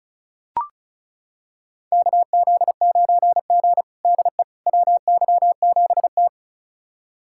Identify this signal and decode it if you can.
Morse code, heard as a keyed tone, about 36 wpm, 700 Hz. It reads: KZ9G DE WY7T
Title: USA Callsign DE Callsign - 4 Characters - 35wpm